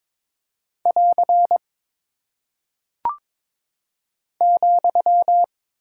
Morse code, heard as a keyed tone, about 23 wpm, 700 Hz. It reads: <AR> ,
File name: practice / 22wpm / Single Punctuation-Prosign - 22wpm